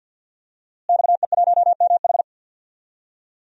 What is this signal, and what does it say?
Morse code, heard as a keyed tone, about 46 wpm, 700 Hz. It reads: XE1MH
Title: DXCC-100 Call Signs - 5 Characters - Repeated 3x - 50wpm